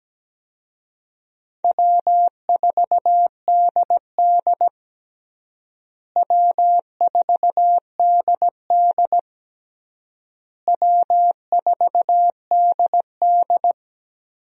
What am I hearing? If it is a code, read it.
Morse code, heard as a keyed tone, about 17 wpm, 700 Hz. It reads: W4DD W4DD W4DD
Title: USA Call Signs - 4 Characters - Repeated 3x - 17wpm